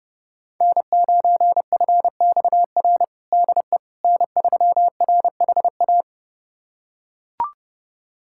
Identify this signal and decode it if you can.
Morse code, heard as a keyed tone, about 30 wpm, 700 Hz. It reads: N9FXR DE N3RHA
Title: USA Callsign DE Callsign - 5 Characters - 30wpm